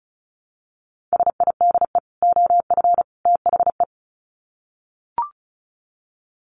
Morse code, heard as a keyed tone, about 34 wpm, 700 Hz. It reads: SIDE OF THE
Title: Sets of 3 Words 35wpm